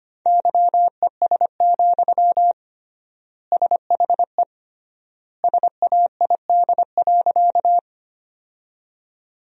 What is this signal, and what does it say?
Morse code, heard as a keyed tone, about 25 wpm, 700 Hz. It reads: YES, SHE SAID.
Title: 3x Word-Spacing - Sentences from Top 500 Words - 25wpm